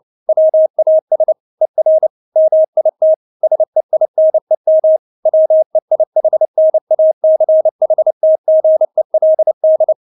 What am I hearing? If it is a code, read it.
Morse code, heard as a keyed tone, about 29 wpm, 630 Hz. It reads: WAS ER MIT SEINEM WEIHNACHTGELD
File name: Witz10 29wpm